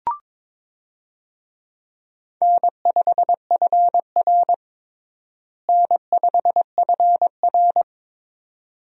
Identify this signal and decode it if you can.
Morse code, heard as a keyed tone, about 22 wpm, 700 Hz. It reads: N5FR N5FR
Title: USA Call Signs - 4 Characters - Repeated 2x - 22wpm